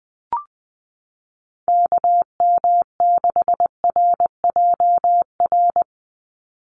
Morse code, heard as a keyed tone, about 20 wpm, 700 Hz. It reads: KM6RJR
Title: USA Call Signs 20wpm